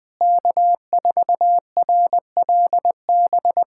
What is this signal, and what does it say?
Morse code, heard as a keyed tone, about 20 wpm, 700 Hz. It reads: K4RLB
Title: USA Callsign DE Callsign 20wpm